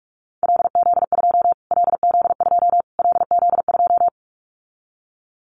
Morse code, heard as a keyed tone, about 44 wpm, 700 Hz. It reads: LZ2 LZ2 LZ2
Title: Callsign Prefixes for Top 25 DXCC - Repeated 3x - 45wpm